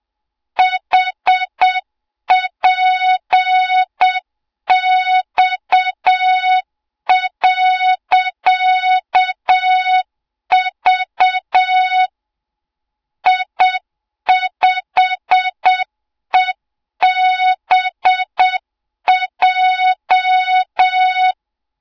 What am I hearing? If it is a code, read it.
Morse code, heard as a keyed tone, about 7 wpm, 3700 Hz, with noise. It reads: HPX.V I5EBJ
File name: Morze-HA2MN / hang / 20 GYAKORLAT